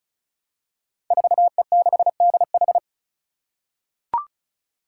Morse code, heard as a keyed tone, about 36 wpm, 700 Hz. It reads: 4E6DH
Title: Five Challenging Letter-Number Combinations 35wpm